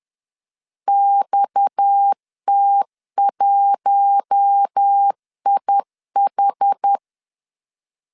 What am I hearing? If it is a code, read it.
Morse code, heard as a keyed tone, about 10 wpm, 800 Hz. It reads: XT1IH
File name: LEZIONI / 8 YJ45 / 8 Y 10wpm 2615